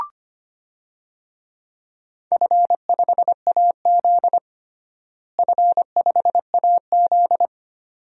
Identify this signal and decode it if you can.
Morse code, heard as a keyed tone, about 25 wpm, 700 Hz. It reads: F5AZ F5AZ
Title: DXCC-100 Call Signs - 4 Characters - Repeated 2x - 25wpm